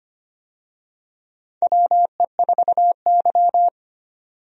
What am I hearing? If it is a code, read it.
Morse code, heard as a keyed tone, about 25 wpm, 700 Hz. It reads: WE4Y
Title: USA Call Signs - 4 Characters - 25wpm